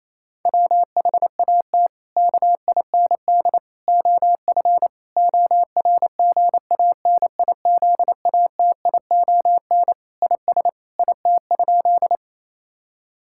Morse code, heard as a keyed tone, about 28 wpm, 700 Hz. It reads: WHAT KIND OF ORGANIZATION IS IT?
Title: Sentences from Top 1000 Words - Review Part 5 - 28wpm